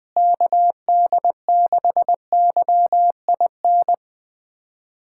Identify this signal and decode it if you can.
Morse code, heard as a keyed tone, about 20 wpm, 700 Hz. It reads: KD6YIN
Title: USA Callsign DE Callsign 20wpm